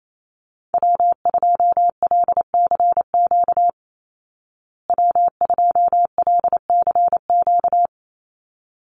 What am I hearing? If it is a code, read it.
Morse code, heard as a keyed tone, about 28 wpm, 700 Hz. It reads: W2LCQ W2LCQ
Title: Call Signs Repeated 3x - 28wpm